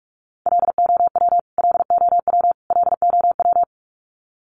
Morse code, heard as a keyed tone, about 44 wpm, 700 Hz. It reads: LOW LOW LOW
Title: Top 500 Words Repeated 45wpm